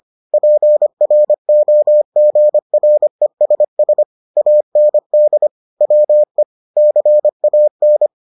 Morse code, heard as a keyed tone, about 25 wpm, 600 Hz. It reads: PROGRESS AND WE CAN